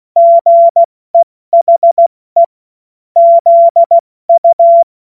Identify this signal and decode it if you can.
Morse code, heard as a keyed tone, about 16 wpm, 680 Hz. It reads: GEHE ZU
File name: Witz-08-Juli-2025-70-F80